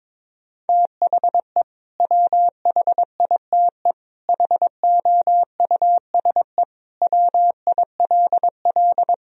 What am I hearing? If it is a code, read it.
Morse code, heard as a keyed tone, about 22 wpm, 700 Hz. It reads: THE WHITE HOUSE WILL
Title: Sentences from Top 1000 Words - Review Part 1 - 22wpm